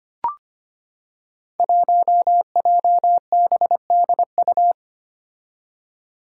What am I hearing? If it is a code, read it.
Morse code, heard as a keyed tone, about 25 wpm, 700 Hz. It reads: 1JBDU
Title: Five Challenging Letter-Number Combinations 25wpm